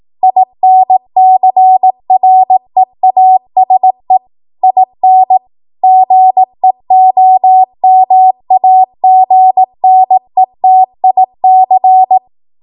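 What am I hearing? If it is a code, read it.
Morse code, heard as a keyed tone, about 18 wpm, 750 Hz. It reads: INCREASE IN GEOMAGNETIC
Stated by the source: Code practice at 18 WPM, a transition run bridging 15 to 20 WPM.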